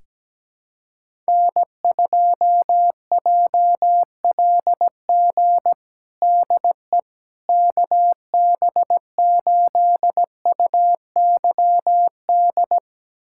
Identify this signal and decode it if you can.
Morse code, heard as a keyed tone, about 17 wpm, 700 Hz. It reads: N2JLG DE KB8UYD
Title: USA Callsign DE Callsign 17wpm